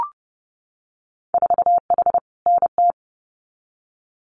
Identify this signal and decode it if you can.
Morse code, heard as a keyed tone, about 30 wpm, 700 Hz. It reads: 4H NT
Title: Winter Field Day Exchanges Encore 30wpm